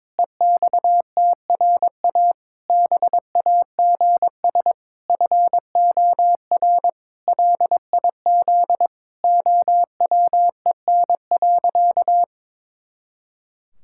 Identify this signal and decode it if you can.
Morse code, heard as a keyed tone, about 21 wpm, 700 Hz. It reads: EXTRA BAGS FOR LIZ OWEN.
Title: Pangram Challenge v2 - 22wpm